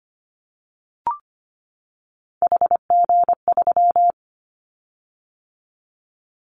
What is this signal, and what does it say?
Morse code, heard as a keyed tone, about 25 wpm, 700 Hz. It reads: HG3
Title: Callsign Prefixes for Top 25 DXCC - 25wpm